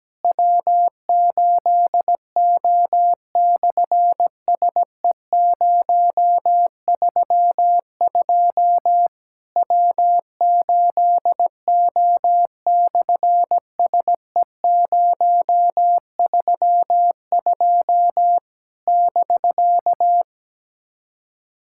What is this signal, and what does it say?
Morse code, heard as a keyed tone, about 17 wpm, 700 Hz. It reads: W8O/SE032 W8O/SE032 <BK>
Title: SOTA QSO Practice 17wpm